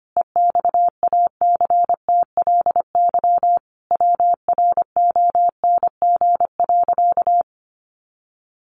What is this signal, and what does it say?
Morse code, heard as a keyed tone, about 25 wpm, 700 Hz. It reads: EXACTLY WRONG.
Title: Sentences from Top 900 Words 25wpm